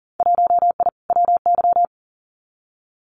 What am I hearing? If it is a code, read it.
Morse code, heard as a keyed tone, about 41 wpm, 700 Hz. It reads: 1I WY
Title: Winter Field Day Exchanges Encore 40wpm